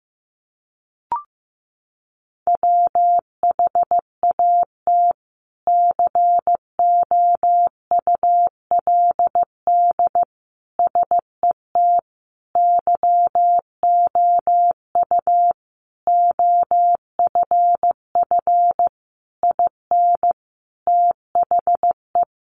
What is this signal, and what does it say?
Morse code, heard as a keyed tone, about 15 wpm, 700 Hz. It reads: WHAT COULD SET YOU OFF IN THE
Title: Sentences from Top 200 Words 15wpm